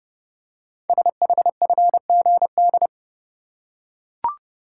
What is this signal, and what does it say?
Morse code, heard as a keyed tone, about 30 wpm, 700 Hz. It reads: SHFGD
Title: Code Groups - 5 Letters - 30wpm